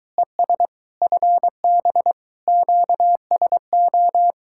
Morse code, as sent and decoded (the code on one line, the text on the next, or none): . ... / ..-. -... / --.- ... ---
ES FB QSO